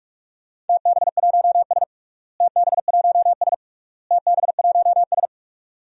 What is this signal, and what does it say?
Morse code, heard as a keyed tone, about 46 wpm, 700 Hz. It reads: TB1S TB1S TB1S
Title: DXCC-100 Call Signs - 4 Characters - Repeated 3x - 45wpm